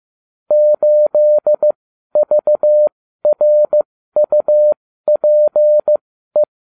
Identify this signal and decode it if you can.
Morse code, heard as a keyed tone, about 15 wpm, spaced as for 12 wpm, 600 Hz. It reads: 8VRUPE